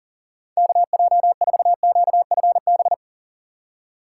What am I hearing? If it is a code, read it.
Morse code, heard as a keyed tone, about 37 wpm, 700 Hz. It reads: KJ4QFB
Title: USA Call Signs - 6 Characters - 2x Character Spacing - 40wpm